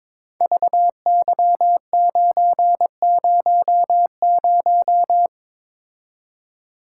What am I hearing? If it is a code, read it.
Morse code, heard as a keyed tone, about 22 wpm, 700 Hz. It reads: VY900
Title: Code Groups - 5 Letter:Numbers - 22wpm